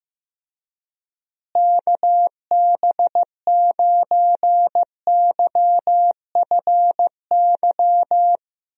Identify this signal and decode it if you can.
Morse code, heard as a keyed tone, about 15 wpm, 700 Hz. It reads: KB9YFY
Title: USA Call Signs - 6 Characters - Repeated 3x - 15wpm